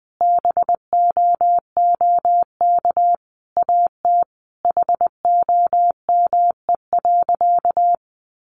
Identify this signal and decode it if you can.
Morse code, heard as a keyed tone, about 20 wpm, 700 Hz. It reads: BOOK AT HOME.
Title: Sentences from Top 1400 Words 20wpm